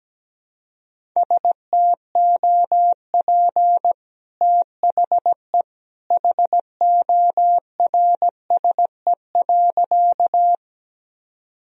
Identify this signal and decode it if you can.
Morse code, heard as a keyed tone, about 17 wpm, 700 Hz. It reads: STOP THE HORSE.
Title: Sentences from Top 1000 Words - Review Part 1 - 17wpm